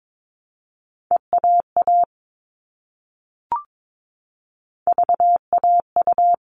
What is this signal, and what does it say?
Morse code, heard as a keyed tone, about 22 wpm, 700 Hz. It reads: EAA VAU
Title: All Cut Numbers - 3-Digit 22wpm